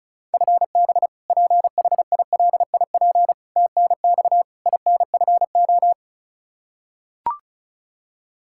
Morse code, heard as a keyed tone, about 36 wpm, 700 Hz. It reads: FB PHILIP TNX INFO
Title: Practice QSO - Exchange 3 - More Info - 35wpm